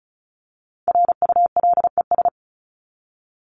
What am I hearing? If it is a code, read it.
Morse code, heard as a keyed tone, about 34 wpm, 700 Hz. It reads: RULES
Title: ES:S Suffix - Top 100 Words with Base Words 35wpm